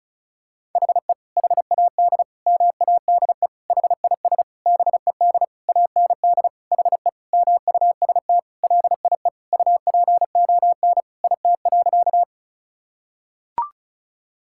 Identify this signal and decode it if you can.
Morse code, heard as a keyed tone, about 36 wpm, 700 Hz. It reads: HE HAD MADE HIS BED AND HE MUST LIE UPON IT.
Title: Sentences from Top 1600 Words 35wpm